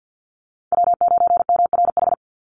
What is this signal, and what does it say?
Morse code, heard as a keyed tone, about 48 wpm, 700 Hz. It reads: W9MRH
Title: NA SOTA Chaser Callsigns 50wpm